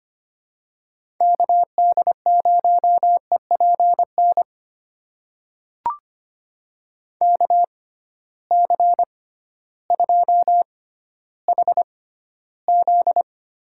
Morse code, heard as a keyed tone, about 25 wpm, spaced as for 13 wpm, 700 Hz. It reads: KD0EPN KC2HZ
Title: USA Call Signs - 6 Characters - 6x Character Spacing - 25wpm